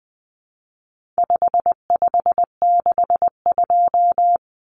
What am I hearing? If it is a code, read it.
Morse code, heard as a keyed tone, about 20 wpm, 700 Hz. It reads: 5562
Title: Common Cut Numbers - 4-Digit 20wpm